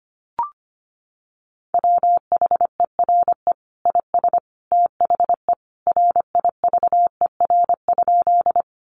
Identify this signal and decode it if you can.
Morse code, heard as a keyed tone, about 25 wpm, 700 Hz. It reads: WHERE IS THE RIVER?
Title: Sentences from Top 300 Words 25wpm